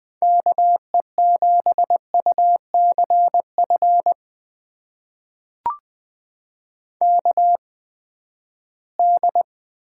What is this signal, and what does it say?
Morse code, heard as a keyed tone, about 20 wpm, spaced as for 14 wpm, 700 Hz. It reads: KE7UCF KD
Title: USA Call Signs - 6 Characters - 8x Character Spacing - 20wpm